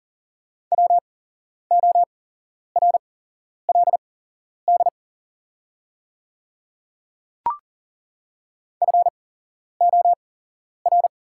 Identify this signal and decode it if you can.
Morse code, heard as a keyed tone, about 40 wpm, spaced as for 11 wpm, 700 Hz. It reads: WORLD FOR
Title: 8x Character Spacing - Top 300 Words - 40wpm